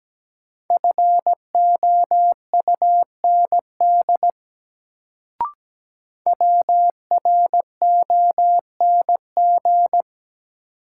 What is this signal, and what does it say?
Morse code, heard as a keyed tone, about 17 wpm, 700 Hz. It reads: FOUND WRONG
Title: Sets of 5 Words Encore - New Word Drill - 17wpm